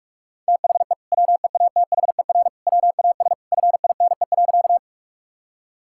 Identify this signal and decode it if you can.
Morse code, heard as a keyed tone, about 46 wpm, 700 Hz. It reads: THE WEATHER WAS FINE.
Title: Sentences from Top 1000 Words - Review Part 5 - 45wpm